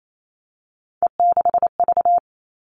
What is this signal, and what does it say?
Morse code, heard as a keyed tone, about 28 wpm, 700 Hz. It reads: E6V
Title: All Cut Numbers - 3-Digit 28wpm